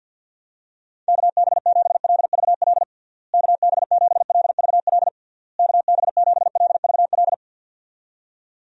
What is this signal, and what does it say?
Morse code, heard as a keyed tone, about 53 wpm, 700 Hz. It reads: KB7LVL KB7LVL KB7LVL
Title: USA Call Signs - 6 Characters - Repeated 3x - 50wpm